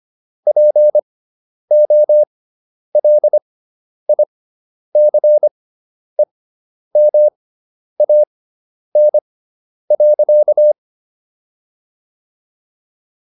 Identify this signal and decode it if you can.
Morse code, heard as a keyed tone, about 25 wpm, spaced as for 10 wpm, 600 Hz. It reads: POLICEMAN.